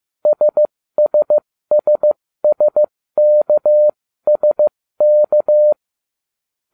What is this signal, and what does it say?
Morse code, heard as a keyed tone, about 15 wpm, spaced as for 13 wpm, 600 Hz. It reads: SSSSKSK